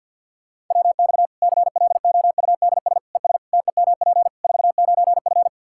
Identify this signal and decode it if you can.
Morse code, heard as a keyed tone, about 48 wpm, 700 Hz. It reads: WX CLOUDS ES TEMP 49F
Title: K1USN SST QSO Practice 50wpm